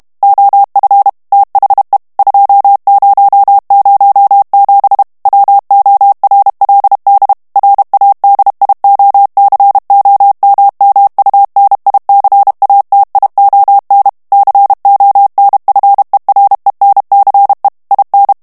Morse code, heard as a keyed tone, about 32 wpm, 800 Hz. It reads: OF THE 2007 WORLD RADIOCOMMUNICATION CONFERENCE IN
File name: ANGLAIS texte clair 3 vitesse 30 mots minute